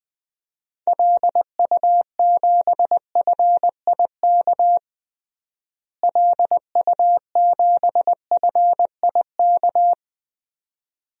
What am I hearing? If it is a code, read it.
Morse code, heard as a keyed tone, about 21 wpm, 700 Hz. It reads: LU7FIK LU7FIK
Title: DXCC-100 Call Signs - 6 Characters - Repeated 2x - 20wpm